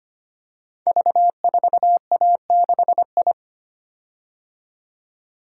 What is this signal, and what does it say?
Morse code, heard as a keyed tone, about 25 wpm, 700 Hz. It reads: V4A6I
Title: Code Groups - 5 Letter:Numbers - 25wpm